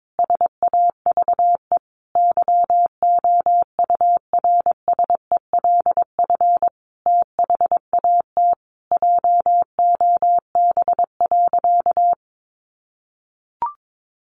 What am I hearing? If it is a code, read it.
Morse code, heard as a keyed tone, about 22 wpm, 700 Hz. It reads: SAVE YOURSELF THAT JOB.
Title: Sentences from Top 1100 Words 22wpm